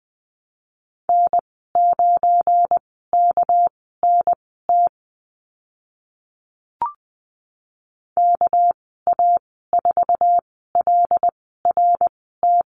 Morse code, heard as a keyed tone, about 20 wpm, spaced as for 14 wpm, 700 Hz. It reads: N9KNT KA4LRT
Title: 207 Lesson 33g - Letter j - Call Signs 20wpm